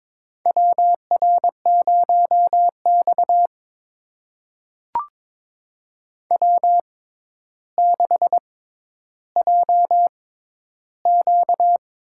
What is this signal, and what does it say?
Morse code, heard as a keyed tone, about 22 wpm, spaced as for 11 wpm, 700 Hz. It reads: WR0X W6JQ
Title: USA Call Signs - 6x Character Spacing - 22wpm